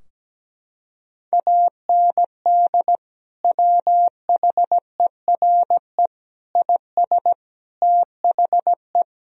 Morse code, heard as a keyed tone, about 17 wpm, 700 Hz. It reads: AND WHERE IS THE
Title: Sentences from Top 1000 Words - Review Part 4 - 17wpm